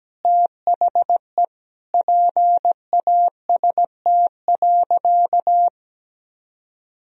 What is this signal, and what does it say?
Morse code, heard as a keyed tone, about 17 wpm, 700 Hz. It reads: THE PAST.
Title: Sentences from Top 1000 Words - Review Part 3 - 17wpm